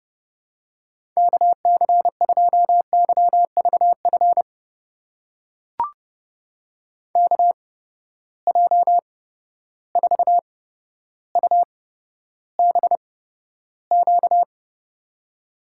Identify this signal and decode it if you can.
Morse code, heard as a keyed tone, about 30 wpm, spaced as for 13 wpm, 700 Hz. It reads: KC2YVF KJ4UBQ
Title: USA Call Signs - 6 Characters - 8x Character Spacing - 30wpm